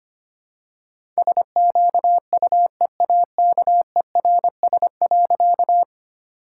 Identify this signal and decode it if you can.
Morse code, heard as a keyed tone, about 25 wpm, 700 Hz. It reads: SQUEAKERS.
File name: practice / 25wpm / All-Age Appropriate Jokes - 25wpm